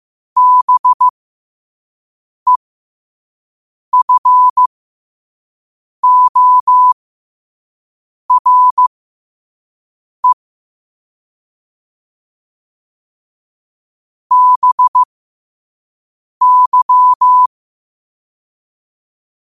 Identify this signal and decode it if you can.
Morse code, heard as a keyed tone, about 15 wpm, spaced as for 5 wpm, 1000 Hz. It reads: BEFORE BY
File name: qotdfiles / 5wpm